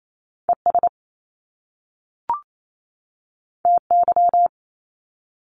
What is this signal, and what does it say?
Morse code, heard as a keyed tone, about 28 wpm, 700 Hz. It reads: ES TY
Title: Top Two Letter Combinations in Words 28wpm